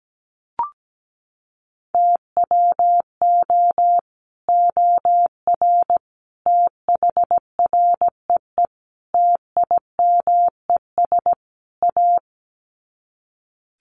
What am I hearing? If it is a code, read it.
Morse code, heard as a keyed tone, about 17 wpm, 700 Hz. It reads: TWO OR THREE TIMES A
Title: Sets of 5 Words Encore2 17wpm